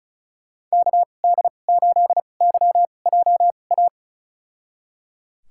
Morse code, heard as a keyed tone, about 34 wpm, spaced as for 25 wpm, 700 Hz. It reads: KD8YJA
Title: USA Call Signs - 6 Characters - 2x Character Spacing - 35wpm